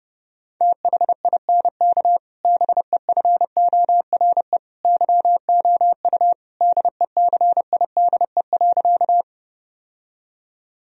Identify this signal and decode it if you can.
Morse code, heard as a keyed tone, about 32 wpm, 700 Hz. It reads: THINK BEFORE YOU DECIDE.
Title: Sentences from Top 1000 Words - Review Part 3 - 30wpm